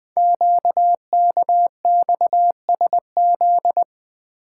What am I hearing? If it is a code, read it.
Morse code, heard as a keyed tone, about 20 wpm, 700 Hz. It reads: QKXSZ
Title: Code Groups - 5 Letters - 20wpm